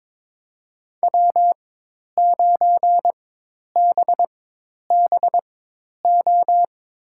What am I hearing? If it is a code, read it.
Morse code, heard as a keyed tone, about 21 wpm, spaced as for 10 wpm, 700 Hz. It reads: W9BBO
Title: USA Call Signs - 5 Characters - 4x Character Spacing - 22wpm